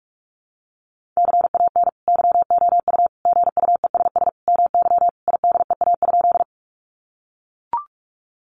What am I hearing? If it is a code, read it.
Morse code, heard as a keyed tone, about 44 wpm, 700 Hz. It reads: CAN YOU GUESS MY IDEA?
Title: Sentences from Top 1500 Words 45wpm